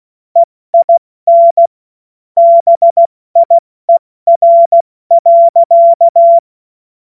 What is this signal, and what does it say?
Morse code, heard as a keyed tone, about 16 wpm, 680 Hz. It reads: EIN BIER.
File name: Morsekurs / Witze / Witz 11. Mai 2021, 70 F80